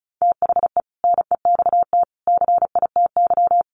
Morse code, heard as a keyed tone, about 34 wpm, 700 Hz. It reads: THE NEXT CITY
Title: Sentences from Top 1400 Words 35wpm